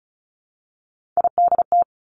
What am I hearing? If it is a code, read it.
Morse code, heard as a keyed tone, about 36 wpm, 700 Hz. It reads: IDT
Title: Three Challenging Letter-Number Combinations 35wpm